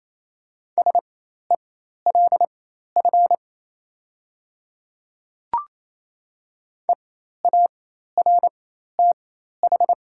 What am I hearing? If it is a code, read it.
Morse code, heard as a keyed tone, about 28 wpm, spaced as for 13 wpm, 700 Hz. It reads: SELF EARTH
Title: 4x Character Spacing - Top 300 Words - 28wpm